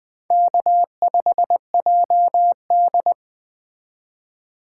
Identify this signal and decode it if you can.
Morse code, heard as a keyed tone, about 20 wpm, 700 Hz. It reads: K5JD
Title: Speed Racing - USA 1x2 Call Signs 17wpm